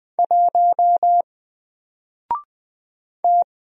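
Morse code, heard as a keyed tone, about 20 wpm, 700 Hz. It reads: 1 T
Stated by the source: Single letter and number Morse drill at 20 wpm, from N4LCD.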